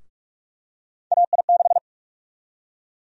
Morse code, heard as a keyed tone, about 44 wpm, 700 Hz. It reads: AI6
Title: USA Call Sign Prefixes - Repeated 2x - 45wpm